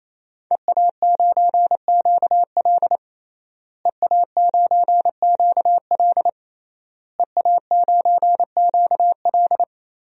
Morse code, heard as a keyed tone, about 28 wpm, 690 Hz. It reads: EA9QL EA9QL EA9QL
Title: DXCC-100 Call Signs - 5 Characters - Repeated 3x - 28wpm